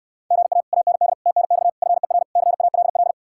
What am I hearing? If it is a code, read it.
Morse code, heard as a keyed tone, about 67 wpm, 700 Hz. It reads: GA AND TNX FER CALL